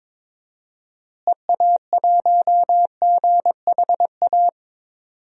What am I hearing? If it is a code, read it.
Morse code, heard as a keyed tone, about 22 wpm, 700 Hz. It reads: EA1GHA
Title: DXCC-100 Call Signs - 6 Characters - Repeated 3x - 22wpm